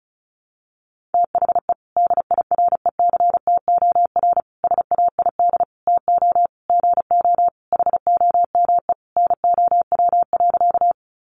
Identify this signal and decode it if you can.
Morse code, heard as a keyed tone, about 34 wpm, 700 Hz. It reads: THE DIRECTOR SAID TO GO HOME NOW.
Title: Sentences from Top 1300 Words 35wpm